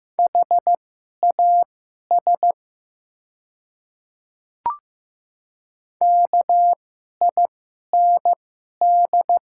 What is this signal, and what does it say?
Morse code, heard as a keyed tone, about 15 wpm, spaced as for 11 wpm, 700 Hz. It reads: HAS KIND
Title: 2x Character Spacing - Top 200 Words - 15wpm